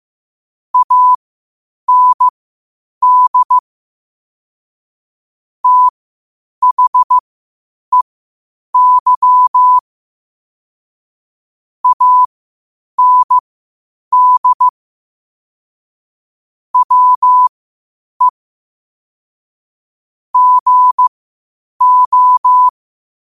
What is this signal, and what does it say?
Morse code, heard as a keyed tone, about 15 wpm, spaced as for 8 wpm, 1000 Hz. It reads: AND THEY AND WE GO
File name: qotdfiles / 8wpm